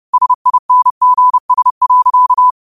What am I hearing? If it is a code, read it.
Morse code, heard as a keyed tone, about 30 wpm, 1000 Hz. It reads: SINGS.
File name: qotdfiles / 30wpm